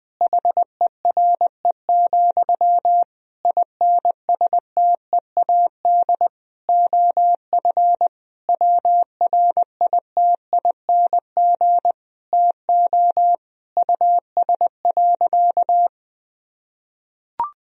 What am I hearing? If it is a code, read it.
Morse code, heard as a keyed tone, about 20 wpm, 700 Hz. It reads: HERE, INSTEAD OF WRITING TO US.
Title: Sentences from Top 1000 Words - Review Part 4 - 20wpm